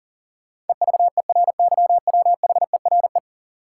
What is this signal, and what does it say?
Morse code, heard as a keyed tone, about 40 wpm, 700 Hz. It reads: EVERYWHERE
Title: Sentences from Top 1600 Words - New Word Drill - 40wpm